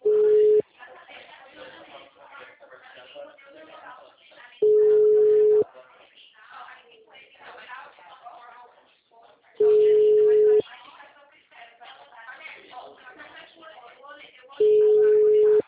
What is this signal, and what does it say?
Ringback tone (Europe, 425 Hz), with speech.
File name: out- -2032- - - .73430